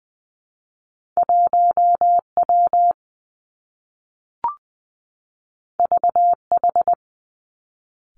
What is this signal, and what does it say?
Morse code, heard as a keyed tone, about 20 wpm, 700 Hz. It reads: 1 W V H
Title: Two Challenging Letter-Number Combinations 20wpm